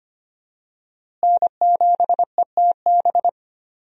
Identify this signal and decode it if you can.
Morse code, heard as a keyed tone, about 25 wpm, 700 Hz. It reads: N7ETB
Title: USA Call Signs - 5 Characters - Repeated 2x - 25wpm